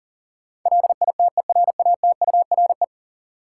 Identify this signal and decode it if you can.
Morse code, heard as a keyed tone, about 40 wpm, 700 Hz. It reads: LITERATURE
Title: Sentences from Top 600 Words - New Word Drill - 40wpm